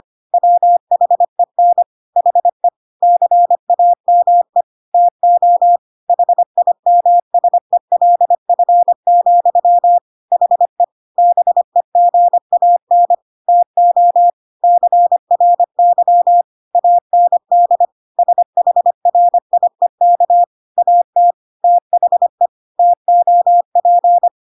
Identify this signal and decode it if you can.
Morse code, heard as a keyed tone, about 25 wpm, 700 Hz. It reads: WHEN HE CAME TO HIMSELF, HE BEGAN TO CRY AND SHRIEK AT THE TOP
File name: Pinocchio / Pinocchio ch5 25 25 ik1hge